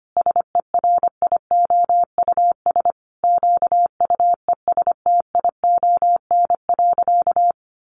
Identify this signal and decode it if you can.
Morse code, heard as a keyed tone, about 25 wpm, 700 Hz. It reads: SERIOUS QUESTION.
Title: Sentences from Top 1400 Words 25wpm